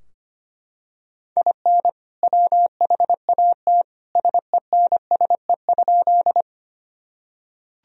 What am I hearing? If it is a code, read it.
Morse code, heard as a keyed tone, about 25 wpm, 700 Hz. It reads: IN WHAT SENSE?
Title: Sentences from Top 1000 Words - Review Part 4 - 25wpm